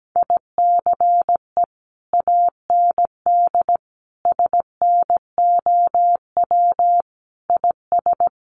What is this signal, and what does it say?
Morse code, heard as a keyed tone, about 17 wpm, 700 Hz. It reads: ICE AND SNOW IS
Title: Sentences from Top 1000 Words 17wpm